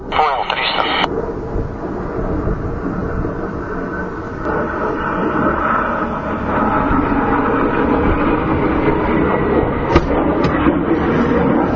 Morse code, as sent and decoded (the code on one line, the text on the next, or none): none